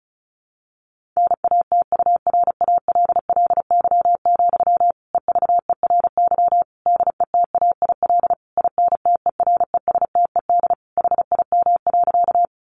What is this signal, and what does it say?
Morse code, heard as a keyed tone, about 34 wpm, 700 Hz. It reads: NATURALLY, EVERY DETAIL INTERESTED HIM.
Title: Sentences from Top 1500 Words 35wpm